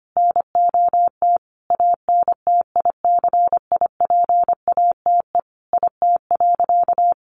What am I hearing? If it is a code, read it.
Morse code, heard as a keyed tone, about 25 wpm, 700 Hz. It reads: NOT ANTICIPATE IT.
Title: Sentences from Top 1500 Words 25wpm